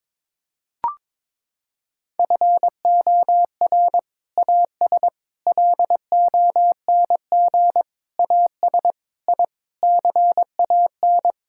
Morse code, heard as a keyed tone, about 22 wpm, 700 Hz. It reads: FOR AS LONG AS I CAN
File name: Sentences with 7-Word Phrases - 22wpm